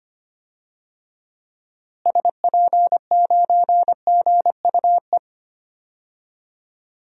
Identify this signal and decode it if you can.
Morse code, heard as a keyed tone, about 25 wpm, 700 Hz. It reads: SP9GUE
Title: DXCC-100 Call Signs - 6 Characters - Repeated 3x - 25wpm